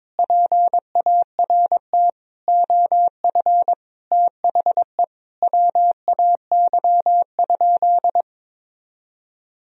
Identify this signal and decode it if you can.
Morse code, heard as a keyed tone, about 22 wpm, 700 Hz. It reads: PART OF THE WAY?
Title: Sentences from Top 1000 Words - Review Part 1 - 22wpm